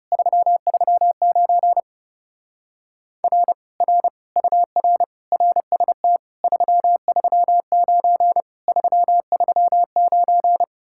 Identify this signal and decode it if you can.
Morse code, heard as a keyed tone, about 30 wpm, 700 Hz. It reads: 339 R R UR RST 339 339
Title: Speed-Racing Common QSO Phrases - 30wpm